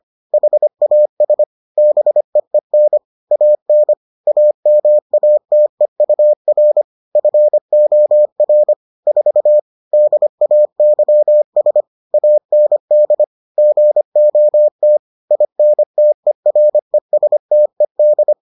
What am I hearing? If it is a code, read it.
Morse code, heard as a keyed tone, about 25 wpm, 600 Hz. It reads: HAS BEEN AN AMATEUR FOR 4 DAYS AND GOT INTERESTED